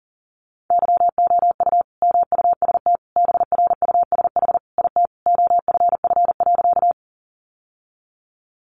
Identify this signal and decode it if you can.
Morse code, heard as a keyed tone, about 40 wpm, 700 Hz. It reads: YOU MUST BRUSH IT OFF.
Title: Sentences from Top 1100 Words 40wpm